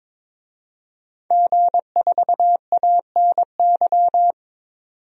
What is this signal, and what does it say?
Morse code, heard as a keyed tone, about 22 wpm, 700 Hz. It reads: G4ANY
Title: DXCC-100 Call Signs - 5 Characters - 22wpm